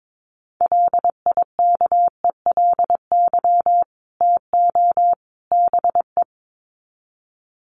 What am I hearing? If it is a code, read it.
Morse code, heard as a keyed tone, about 22 wpm, 700 Hz. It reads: LIKELY TO BE
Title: Sets of 3 Words 22wpm